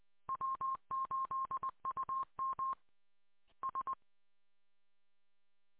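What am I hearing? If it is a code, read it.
Morse code, heard as a keyed tone, about 20 wpm, 1060 Hz. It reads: W8UM S
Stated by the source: Recording from the audio archive of station W8UM.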